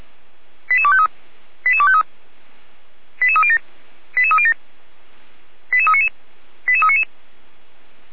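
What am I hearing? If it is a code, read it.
Five-tone selective call (ZVEI), heard as quick runs of five tones, tones of about 80 ms. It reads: 88253 88253 88307 88307 88399 88399
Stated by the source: ZVEI five-tone selective call received on SDR, recorded by Feuerwehr Orlamünde.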